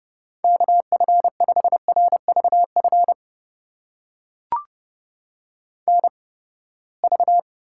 Morse code, heard as a keyed tone, about 30 wpm, spaced as for 21 wpm, 700 Hz. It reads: KF5RVF NV
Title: USA Call Signs - 8x Character Spacing - 30wpm